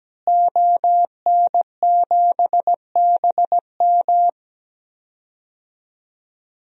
Morse code, heard as a keyed tone, about 17 wpm, 700 Hz. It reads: ON7BM
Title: DXCC-100 Call Signs - 5 Characters - Repeated 3x - 17wpm